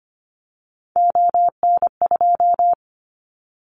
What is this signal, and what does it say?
Morse code, heard as a keyed tone, about 25 wpm, 700 Hz. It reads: ON2
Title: Callsign Prefixes for Top 25 DXCC - Repeated 3x - 25wpm